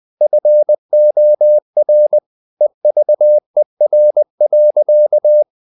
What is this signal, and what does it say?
Morse code, heard as a keyed tone, about 20 wpm, 600 Hz. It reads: FOR EVER.
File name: the old man and the sea part 2 20wpm